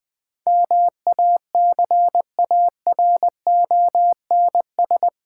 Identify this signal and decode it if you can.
Morse code, heard as a keyed tone, about 20 wpm, 690 Hz. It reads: MACARONS
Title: Difficult Words with Hints - 20wpm